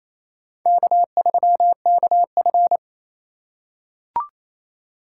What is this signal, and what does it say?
Morse code, heard as a keyed tone, about 27 wpm, 700 Hz. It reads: K3KF
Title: USA Call Signs - 2x Character Spacing - 28wpm